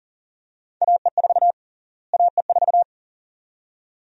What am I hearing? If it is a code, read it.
Morse code, heard as a keyed tone, about 40 wpm, 700 Hz. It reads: AE4 AE4
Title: USA Call Sign Prefixes - Repeated 2x - 40wpm